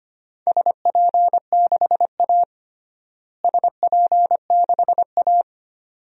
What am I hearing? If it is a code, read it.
Morse code, heard as a keyed tone, about 25 wpm, 700 Hz. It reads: SP6A SP6A
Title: DXCC-100 Call Signs - 4 Characters - Repeated 2x - 25wpm